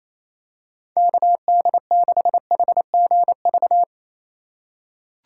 Morse code, heard as a keyed tone, about 28 wpm, 700 Hz. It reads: KD6HGV
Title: USA Call Signs - 6 Characters - 8x Character Spacing - 28wpm